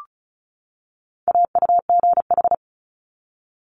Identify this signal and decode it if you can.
Morse code, heard as a keyed tone, about 36 wpm, 700 Hz. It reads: AUGH
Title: Multi-letter Phonemes 35wpm